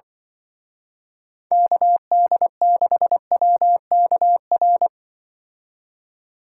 Morse code, heard as a keyed tone, about 24 wpm, 700 Hz. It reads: KD6WKR
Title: Speed Racing - USA 2x3 Call Signs 20wpm